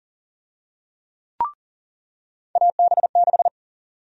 Morse code, heard as a keyed tone, about 40 wpm, 700 Hz. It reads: AB6
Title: USA Call Sign Prefixes - 40wpm